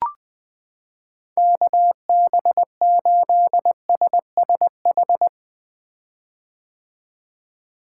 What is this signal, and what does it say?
Morse code, heard as a keyed tone, about 21 wpm, 700 Hz. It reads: KB8SSH
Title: 2023 POTA Hunter Call Signs - 20wpm